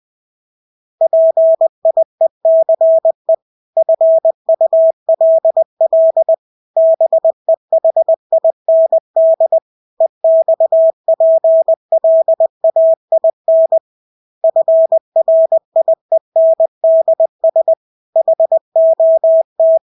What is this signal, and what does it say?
Morse code, heard as a keyed tone, about 20 wpm, 650 Hz. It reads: PIECE FULL BEHIND EXPLAIN FRIENDS HOT
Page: Engelske ord 20 wpm